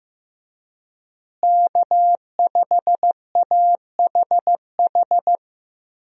Morse code, heard as a keyed tone, about 16 wpm, 700 Hz. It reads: K5AHH
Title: USA Call Signs - 5 Characters - Repeated 3x - 15wpm